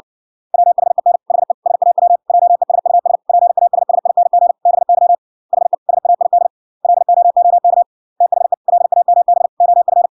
Morse code, heard as a keyed tone, about 57 wpm, 700 Hz. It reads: WHEN HE SAW JESUS PASSING BY HE SAID LOOK THE LAMB OF